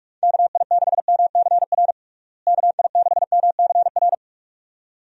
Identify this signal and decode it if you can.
Morse code, heard as a keyed tone, about 46 wpm, 700 Hz. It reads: KI6MCR KI6MCR
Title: USA Call Signs - 6 Characters - Repeated 2x - 45wpm